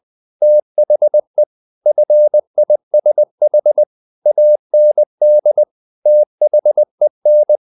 Morse code, heard as a keyed tone, about 20 wpm, 600 Hz. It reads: THE FISH AND THEN
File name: the old man and the sea part 2 20wpm